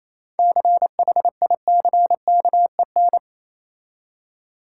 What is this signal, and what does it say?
Morse code, heard as a keyed tone, about 27 wpm, 690 Hz. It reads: CHICKEN
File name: Easy Words - Speed-Racing with Hints - 22wpm